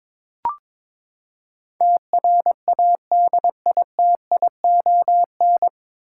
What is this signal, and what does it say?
Morse code, heard as a keyed tone, about 22 wpm, 700 Hz. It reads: TRADITION
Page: Morse Code Ninja practice 22wpm